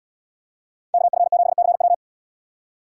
Morse code, heard as a keyed tone, about 72 wpm, 700 Hz. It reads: KF7CW